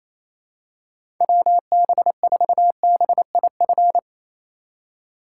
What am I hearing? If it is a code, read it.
Morse code, heard as a keyed tone, about 28 wpm, 700 Hz. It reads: WB4BIF
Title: USA Call Signs - 6 Characters - 3x Character Spacing - 28wpm